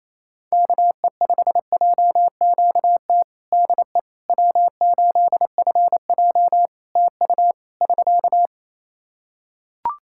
Morse code, heard as a keyed tone, about 27 wpm, 700 Hz. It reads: KE5JQT DE W8FJ TU <SK>
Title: Practice QSO - Exchange 4 - Ending - 28wpm